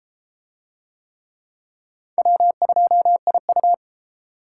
Morse code, heard as a keyed tone, about 30 wpm, 700 Hz. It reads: W2IU